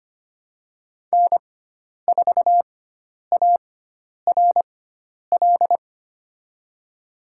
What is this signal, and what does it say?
Morse code, heard as a keyed tone, about 25 wpm, spaced as for 10 wpm, 700 Hz. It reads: N4ARL
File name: Lesson 16g - Letter c - Call Signs 25wpm